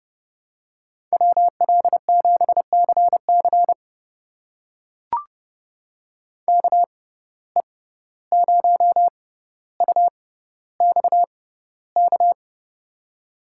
Morse code, heard as a keyed tone, about 30 wpm, spaced as for 15 wpm, 700 Hz. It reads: WL7CC KE0UXK
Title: USA Call Signs - 6x Character Spacing - 30wpm